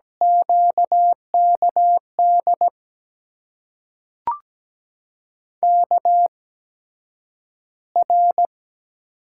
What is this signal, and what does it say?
Morse code, heard as a keyed tone, about 17 wpm, spaced as for 9 wpm, 700 Hz. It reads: QKD KR
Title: USA Call Signs - 6 Characters - 8x Character Spacing - 17wpm